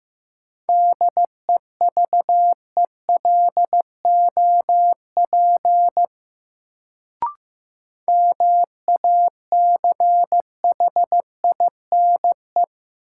Morse code, heard as a keyed tone, about 15 wpm, 700 Hz. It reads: DEVELOP MACHINE
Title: Sentences from Top 1000 Words - Review Part 2 - New Word Drill - 15wpm